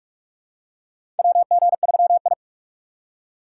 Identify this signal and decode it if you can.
Morse code, heard as a keyed tone, about 42 wpm, 700 Hz. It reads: WG3I
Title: USA Call Signs - 4x Character Spacing - 45wpm